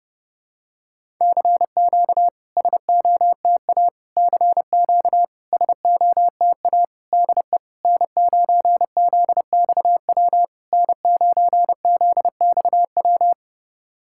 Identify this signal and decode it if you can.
Morse code, heard as a keyed tone, about 32 wpm, 700 Hz. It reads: CQ SOTA CQ SOTA DE N9ZXW N9ZXW
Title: SOTA QSO Practice 30wpm